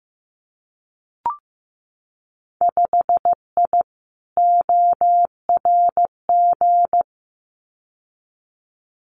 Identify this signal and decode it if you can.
Morse code, heard as a keyed tone, about 15 wpm, 700 Hz. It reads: 5I ORG
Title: Winter Field Day Exchanges Encore 15wpm